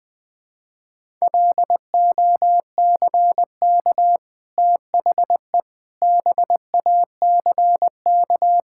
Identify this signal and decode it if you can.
Morse code, heard as a keyed tone, about 20 wpm, 700 Hz. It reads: LOCK THE BACK
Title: Easy Sentences with Hints - 20wpm